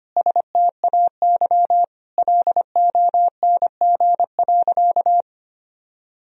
Morse code, heard as a keyed tone, about 25 wpm, 700 Hz. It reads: STAY LONG.
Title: Sentences from Top 1000 Words - Review Part 3 - 25wpm